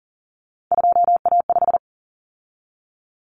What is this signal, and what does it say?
Morse code, heard as a keyed tone, about 40 wpm, 700 Hz. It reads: 2A5
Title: Common Cut Numbers - 3-Digit 40wpm